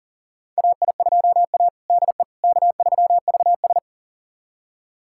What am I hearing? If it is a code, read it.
Morse code, heard as a keyed tone, about 40 wpm, 700 Hz. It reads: AI2A DE K3VS
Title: USA Callsign DE Callsign - 4 Characters - 40wpm